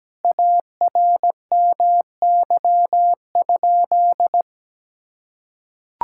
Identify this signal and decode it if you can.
Morse code, heard as a keyed tone, about 17 wpm, 700 Hz. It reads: ARMY?
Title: Sentences from Top 1000 Words - Review Part 5 - 17wpm